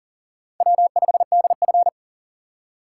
Morse code, heard as a keyed tone, about 40 wpm, 700 Hz. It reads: W5DF
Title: USA Call Signs - 4 Characters - 3x Character Spacing - 40wpm